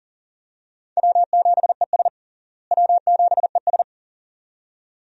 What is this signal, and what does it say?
Morse code, heard as a keyed tone, about 37 wpm, 700 Hz. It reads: W7ES W7ES
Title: USA Call Signs - 4 Characters - Repeated 2x - 40wpm